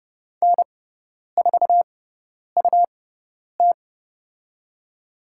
Morse code, heard as a keyed tone, about 30 wpm, spaced as for 10 wpm, 700 Hz. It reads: N4UT
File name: Lesson 18g - Letter m - Call Signs 30wpm